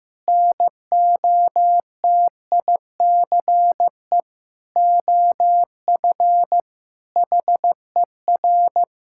Morse code, heard as a keyed tone, about 16 wpm, 700 Hz. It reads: NOTICE OF HER
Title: Sentences from Top 1000 Words - Review Part 6 - 15wpm